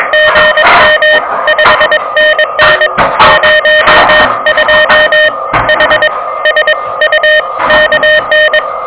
Morse code, heard as a keyed tone, about 22 wpm, 610 Hz, with noise. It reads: GM 5NN 02 H S U KN
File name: OK QRP 2023 / OK QRP 2023 OK1DWF